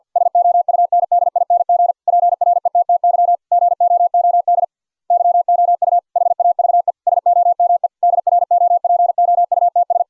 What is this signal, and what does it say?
Morse code, heard as a keyed tone, about 49 wpm, 690 Hz. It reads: SOUNDING PRETTY GOOD YOU HAVE SOME DROPOUTS